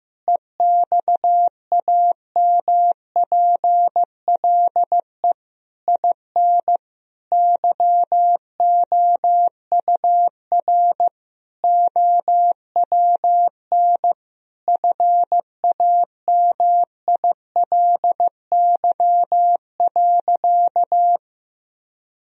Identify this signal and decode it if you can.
Morse code, heard as a keyed tone, about 15 wpm, 700 Hz. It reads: EXAMPLE IN YOUR OWN FAMILY.
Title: Sentences from Top 1000 Words - Review Part 1 - 15wpm